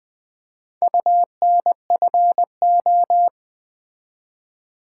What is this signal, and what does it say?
Morse code, heard as a keyed tone, about 20 wpm, 700 Hz. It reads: UNFO
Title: Build A Word - 2 Letters At A Time - 6-Letter Words - 20wpm